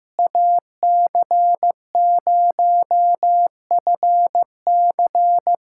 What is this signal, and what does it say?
Morse code, heard as a keyed tone, about 15 wpm, 700 Hz. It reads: AC0FC
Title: USA Callsign DE Callsign - 5 Characters - 15wpm